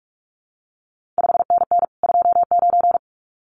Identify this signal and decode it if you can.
Morse code, heard as a keyed tone, about 46 wpm, 700 Hz. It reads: 5NN 29
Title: CQ WW DX Contest Exchange Encore - 45wpm